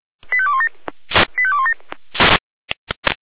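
Five-tone selective call (ZVEI), heard as quick runs of five tones, tones of about 70 ms. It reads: 85217 85217
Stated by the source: ZVEI five-tone selective call received by SDR, code 85217.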